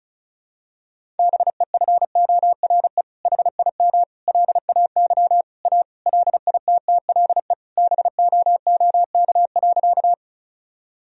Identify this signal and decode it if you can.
Morse code, heard as a keyed tone, about 36 wpm, 700 Hz. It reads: BEFORE HIM LAY A LITTLE BOOK.
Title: Sentences from Top 1000 Words - Review Part 2 - 35wpm